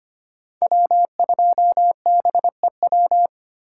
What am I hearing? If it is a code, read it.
Morse code, heard as a keyed tone, about 25 wpm, 700 Hz. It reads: W2BEW
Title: USA Call Signs - 5 Characters - 8x Character Spacing - 25wpm